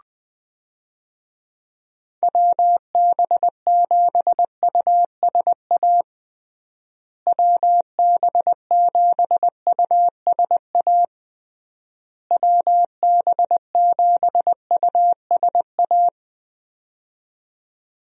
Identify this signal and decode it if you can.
Morse code, heard as a keyed tone, about 20 wpm, 700 Hz. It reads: WB7USA WB7USA WB7USA
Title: USA Call Signs - 6 Characters - Repeated 3x - 20wpm